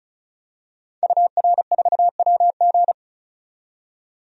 Morse code, heard as a keyed tone, about 36 wpm, 700 Hz. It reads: UR4WG
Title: DXCC-100 Call Signs - 5 Characters - 35wpm